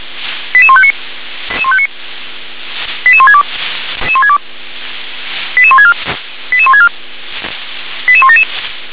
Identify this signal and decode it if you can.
Five-tone selective call (ZVEI), heard as quick runs of five tones, tones of about 70 ms. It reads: 88159 88162 88164 88164 88177
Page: SDR/ZVEI